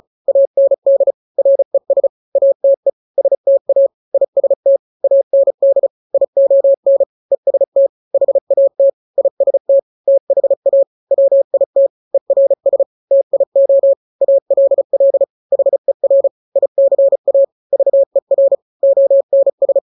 Morse code, heard as a keyed tone, about 34 wpm, 550 Hz. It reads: AND RES ATE STA IST AND ION EST HAT IST THA WIT ERS TIO ALL HER ICA VER ONS
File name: 3-grams 35 30 WPM